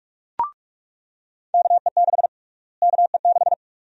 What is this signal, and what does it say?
Morse code, heard as a keyed tone, about 47 wpm, 700 Hz. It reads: KE6 KE6
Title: USA Call Sign Prefixes - Repeated 2x - 45wpm